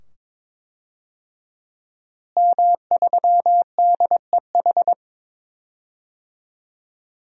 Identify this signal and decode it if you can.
Morse code, heard as a keyed tone, about 22 wpm, 700 Hz. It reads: M3DEH
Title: DXCC-100 Call Signs - 5 Characters - Repeated 2x - 22wpm